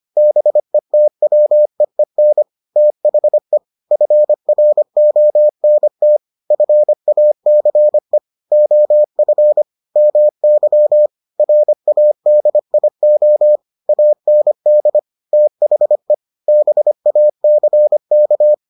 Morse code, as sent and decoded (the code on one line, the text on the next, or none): -... - .-- . -. / - .... / ..-. .-. --- -. - / ..-. .- -.-. / --- ..-. / -- -.-- / .-. .- -.. .. --- / .- -. -.. / - .... / -... .- -.-. -.-
BETWEEN THE FRONT FACE OF MY RADIO AND THE BACK